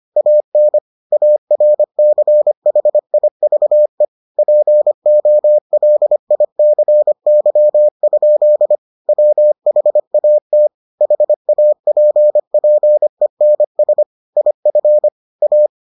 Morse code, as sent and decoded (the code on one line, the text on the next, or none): .- -. / .- .-. -.-. .... .. ...- . / .--. --- .-.. .. -.-. -.-- ..--.. / .-- .... .- - / .... .- .--. .--. -. ... / .. ..-. / .-
AN ARCHIVE POLICY? WHAT HAPPENS IF A